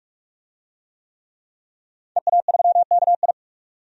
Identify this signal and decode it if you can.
Morse code, heard as a keyed tone, about 43 wpm, 700 Hz. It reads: EA3KI
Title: DXCC-100 Call Signs - 5 Characters - 45wpm